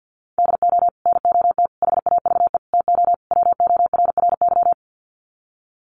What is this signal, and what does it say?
Morse code, heard as a keyed tone, about 50 wpm, 700 Hz. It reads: DO NOT HAVE TO WORRY
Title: Sets of 5 Words 50wpm